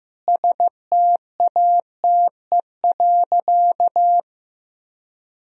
Morse code, heard as a keyed tone, about 15 wpm, 700 Hz. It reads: STATE.
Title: Sentences from Top 1000 Words - Review Part 4 - 15wpm